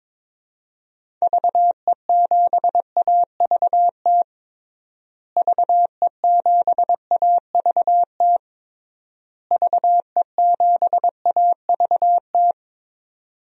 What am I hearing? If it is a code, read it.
Morse code, heard as a keyed tone, about 22 wpm, 700 Hz. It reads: VE7AVT VE7AVT VE7AVT
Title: DXCC-100 Call Signs - 6 Characters - Repeated 3x - 22wpm